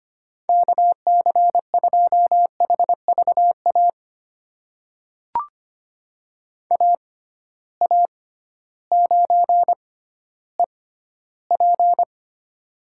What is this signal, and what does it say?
Morse code, heard as a keyed tone, about 25 wpm, spaced as for 14 wpm, 700 Hz. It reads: KC2HVA AA9EP
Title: USA Call Signs - 6x Character Spacing - 25wpm